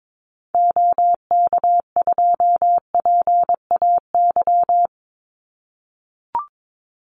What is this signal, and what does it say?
Morse code, heard as a keyed tone, about 22 wpm, 700 Hz. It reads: OK2PAY
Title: Call Signs - 22wpm